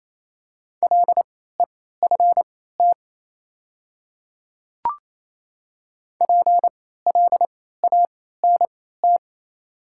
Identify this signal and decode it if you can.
Morse code, heard as a keyed tone, about 28 wpm, spaced as for 16 wpm, 700 Hz. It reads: LEFT PLANT
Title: 3x Character Spacing - Top 300 Words - 28wpm